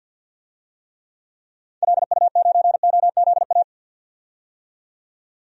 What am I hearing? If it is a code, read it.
Morse code, heard as a keyed tone, about 47 wpm, 700 Hz. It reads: LU9OZA
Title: DXCC-100 Call Signs - 6 Characters - 50wpm